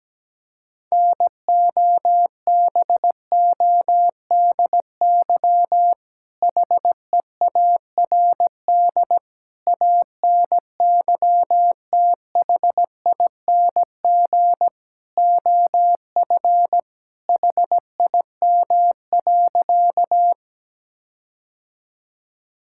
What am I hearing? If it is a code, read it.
Morse code, heard as a keyed tone, about 17 wpm, 700 Hz. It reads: NOBODY HEARD ANYTHING OF HIM.
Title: Sentences from Top 1600 Words 17wpm